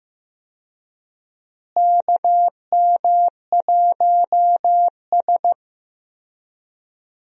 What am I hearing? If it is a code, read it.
Morse code, heard as a keyed tone, about 15 wpm, 700 Hz. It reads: KM1S
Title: USA Call Signs - 4 Characters - Repeated 3x - 15wpm